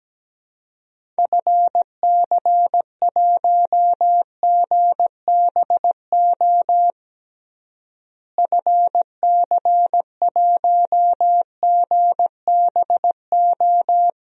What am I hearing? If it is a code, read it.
Morse code, heard as a keyed tone, about 17 wpm, 700 Hz. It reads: FC1GBO FC1GBO
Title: DXCC-100 Call Signs - 6 Characters - Repeated 2x - 17wpm